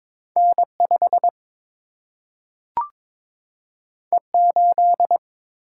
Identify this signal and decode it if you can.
Morse code, heard as a keyed tone, about 22 wpm, 700 Hz. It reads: N 5 E 8
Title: Single-Letter Single-Number 22wpm